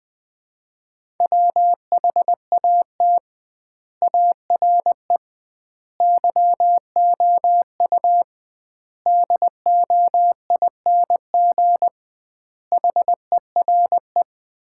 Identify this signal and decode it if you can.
Morse code, heard as a keyed tone, about 21 wpm, 700 Hz. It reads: WHAT ARE YOU DOING HERE
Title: 2x Word-Spacing - Sets of 5 Words - 20wpm